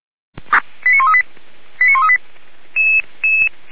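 Five-tone selective call (ZVEI), heard as quick runs of five tones, tones of about 80 ms. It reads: 89138 89138
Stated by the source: ZVEI five-tone call received by SDR, Feuerwehr Orlamünde.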